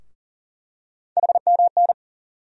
Morse code, heard as a keyed tone, about 40 wpm, 700 Hz. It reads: HMN
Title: USA Call Sign Postfixes - 3 Letters 40wpm